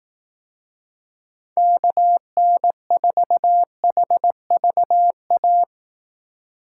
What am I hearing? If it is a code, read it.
Morse code, heard as a keyed tone, about 18 wpm, 700 Hz. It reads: KN4HVA
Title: Speed Racing - USA 2x3 Call Signs 15wpm